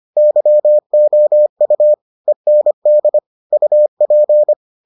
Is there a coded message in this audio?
Yes: Morse code from 0.2 to 4.5 s, heard as a keyed tone, about 25 wpm, 600 Hz. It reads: YOU END UP